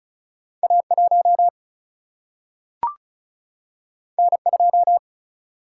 Morse code, heard as a keyed tone, about 36 wpm, 700 Hz. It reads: A 1 N 2
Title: Single-Letter Single-Number 35wpm